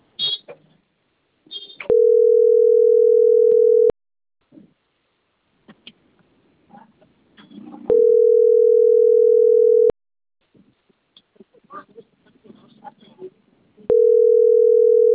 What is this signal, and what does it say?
Ringback tone (North America), 440 + 480 Hz.